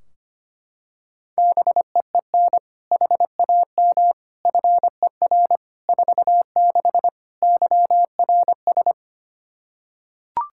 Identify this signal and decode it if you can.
Morse code, heard as a keyed tone, about 25 wpm, 700 Hz. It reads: BEEN HAM FER 46 YRS
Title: K1USN SST QSO Practice 25wpm